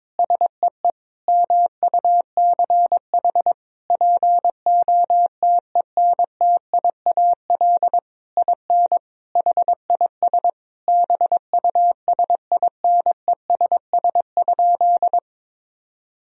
Morse code, heard as a keyed tone, about 22 wpm, 700 Hz. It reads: SEE MUCH POTENTIAL IN HIS BUSINESS?
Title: Sentences from Top 1000 Words - Review Part 4 - 22wpm